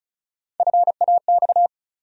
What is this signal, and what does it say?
Morse code, heard as a keyed tone, about 36 wpm, 700 Hz. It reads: FAX
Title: USA Call Sign Postfixes - 3 Letters 35wpm